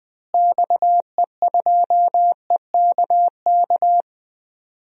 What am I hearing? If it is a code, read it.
Morse code, heard as a keyed tone, about 20 wpm, 700 Hz. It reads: XE2EKK
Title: DXCC-100 Call Signs - 6 Characters - Repeated 2x - 20wpm